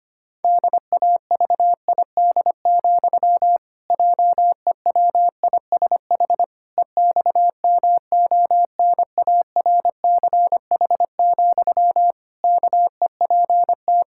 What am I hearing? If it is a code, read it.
Morse code, heard as a keyed tone, about 25 wpm, 700 Hz. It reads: DAVID, JEWISH EXMONARCH, KEPT
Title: Pangram Challenge v2 - 25wpm